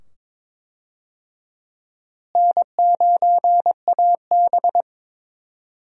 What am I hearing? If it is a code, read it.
Morse code, heard as a keyed tone, about 22 wpm, 700 Hz. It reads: N9AB
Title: USA Call Signs - 4 Characters - Repeated 2x - 22wpm